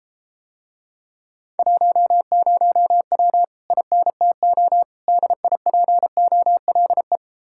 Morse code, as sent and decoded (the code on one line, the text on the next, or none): .---- ----- .-- / .. -. - --- / -.. .. .--. --- .-.. .
10W INTO DIPOLE